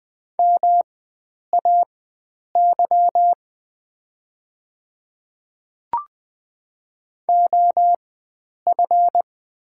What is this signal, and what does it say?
Morse code, heard as a keyed tone, about 20 wpm, spaced as for 9 wpm, 690 Hz. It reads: MAY OF
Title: 4x Character Spacing - Top 100 Words - 20wpm